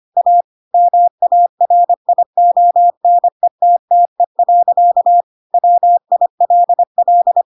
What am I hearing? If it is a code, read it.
Morse code, heard as a keyed tone, about 25 wpm, 700 Hz. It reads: A MARIONETTE. WILL
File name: Pinocchio / Pinocchio ch2 25 25 ik1hge